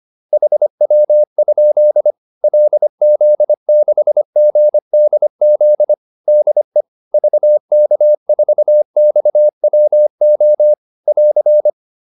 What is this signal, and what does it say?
Morse code, heard as a keyed tone, about 25 wpm, 600 Hz. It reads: HW? LZ6GDZ DE VK4XWO <AR>
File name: QSO407 25